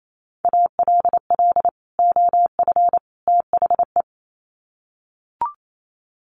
Morse code, heard as a keyed tone, about 28 wpm, 700 Hz. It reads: ALL OF THE
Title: Sets of 3 Words 28wpm